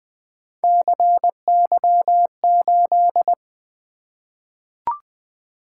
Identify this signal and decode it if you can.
Morse code, heard as a keyed tone, about 20 wpm, 700 Hz. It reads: CY8
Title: Three Letter-Number Combinations - 20wpm